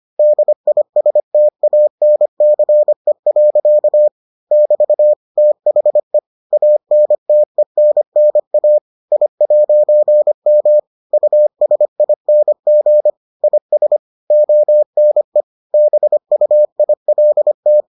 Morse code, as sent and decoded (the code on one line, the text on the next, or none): -.. .. ... - .- -. -.-. .-.-.- / -...- / - .... / .- -. - . -. -. .- / .. .----. -- / ..- ... .. -. --. / .. ... / --- -. / -... ..- .. .-.. -
DISTANCE. <BT> THE ANTENNA I'M USING IS ONE BUILT